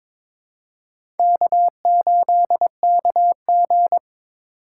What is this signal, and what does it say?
Morse code, heard as a keyed tone, about 21 wpm, 700 Hz. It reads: K8KG
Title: USA Call Signs - 4 Characters - 4x Character Spacing - 22wpm